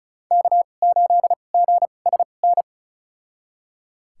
Morse code, heard as a keyed tone, about 36 wpm, spaced as for 25 wpm, 700 Hz. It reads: K8GSN
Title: USA Call Signs - 5 Characters - 2x Character Spacing - 35wpm